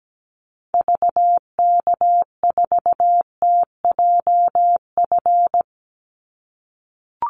Morse code, heard as a keyed tone, about 17 wpm, 700 Hz. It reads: VK4TJF
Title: Call Signs - 17wpm